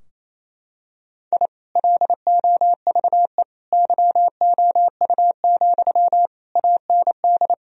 Morse code, heard as a keyed tone, about 28 wpm, 700 Hz. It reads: I LOVE YOU, AND
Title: Sentences from Top 1000 Words - Review Part 2 - 28wpm